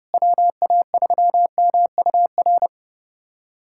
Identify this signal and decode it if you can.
Morse code, heard as a keyed tone, about 32 wpm, 700 Hz. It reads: WA3MUR
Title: USA Callsign DE Callsign - 6 Characters - 30wpm